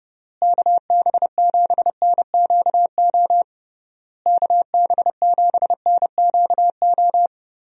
Morse code, heard as a keyed tone, about 32 wpm, 700 Hz. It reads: KB7NQO KB7NQO
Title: USA Call Signs - 6 Characters - Repeated 3x - 30wpm